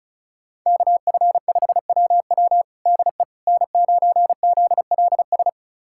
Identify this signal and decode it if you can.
Morse code, heard as a keyed tone, about 36 wpm, 700 Hz. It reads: KF5WW DE N9ZLS
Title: USA Callsign DE Callsign - 5 Characters - 35wpm